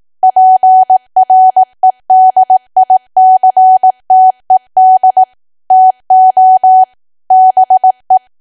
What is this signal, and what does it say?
Morse code, heard as a keyed tone, about 18 wpm, 750 Hz. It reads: PREDICTED TO BE
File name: cwa / W1AW-Level2 / 18WPM